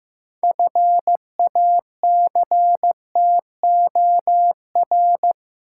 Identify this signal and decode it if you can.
Morse code, heard as a keyed tone, about 15 wpm, 700 Hz. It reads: FACTOR
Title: Six Letter Words - 15wpm